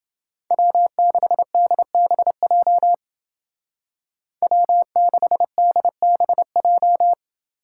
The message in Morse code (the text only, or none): W6DBJ W6DBJ